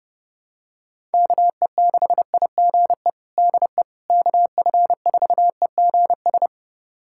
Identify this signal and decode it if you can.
Morse code, heard as a keyed tone, about 29 wpm, 700 Hz. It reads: KE6IGE DE KF4EGS
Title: USA Callsign DE Callsign - 6 Characters - 30wpm